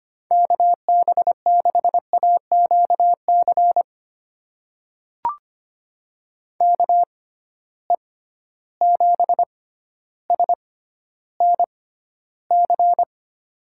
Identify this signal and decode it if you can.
Morse code, heard as a keyed tone, about 25 wpm, spaced as for 13 wpm, 700 Hz. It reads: KB6AQC KE7SNC
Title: USA Call Signs - 6 Characters - 6x Character Spacing - 25wpm